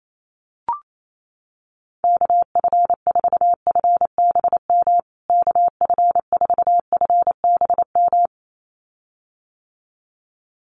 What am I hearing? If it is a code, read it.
Morse code, heard as a keyed tone, about 28 wpm, 700 Hz. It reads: KF4FBM KF4FBM
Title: USA Call Signs Repeated 2x 28wpm